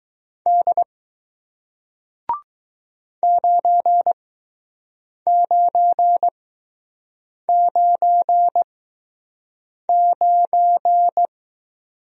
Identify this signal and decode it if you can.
Morse code, heard as a keyed tone, about 20 wpm, spaced as for 6 wpm, 700 Hz. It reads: D 9999